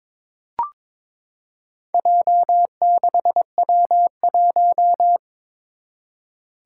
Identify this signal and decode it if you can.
Morse code, heard as a keyed tone, about 22 wpm, 700 Hz. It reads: J6W1
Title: Four Challenging Letter-Number Combinations 22wpm